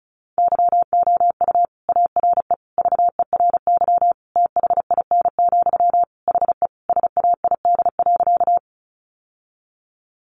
Morse code, heard as a keyed tone, about 34 wpm, 700 Hz. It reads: YOU ARE VERY THIN, HE SAID.
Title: Sentences from Top 1200 Words 35wpm